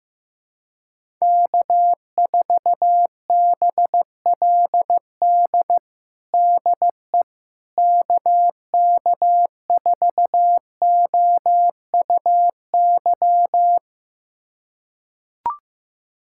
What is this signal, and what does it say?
Morse code, heard as a keyed tone, about 15 wpm, 700 Hz. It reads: K4BLD DE KK4OUY
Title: USA Callsign DE Callsign 15wpm